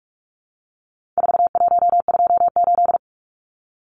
Morse code, heard as a keyed tone, about 44 wpm, 700 Hz. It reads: 4128
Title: Numbers - 4-Digit 45wpm